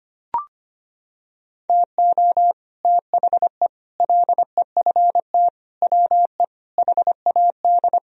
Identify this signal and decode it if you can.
Morse code, heard as a keyed tone, about 25 wpm, 700 Hz. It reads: TO THE LEFT WE HAD
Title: Sentences from Top 1000 Words - Review Part 1 - 25wpm